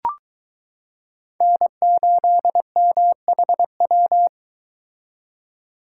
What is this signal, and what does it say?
Morse code, heard as a keyed tone, about 22 wpm, 700 Hz. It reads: N8MHW